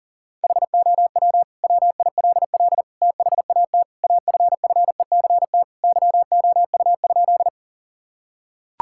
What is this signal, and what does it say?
Morse code, heard as a keyed tone, about 40 wpm, 700 Hz. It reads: HOW WILL THAT AFFECT YOU?
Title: Sentences from Top 1000 Words - Review Part 6 - 40wpm